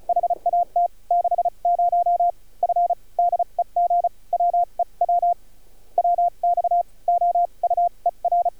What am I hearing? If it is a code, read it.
Morse code, heard as a keyed tone, about 36 wpm, spaced as for 30 wpm, 700 Hz. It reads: HAT 60 F DEG WE W WX OUER
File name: archived / CRSnet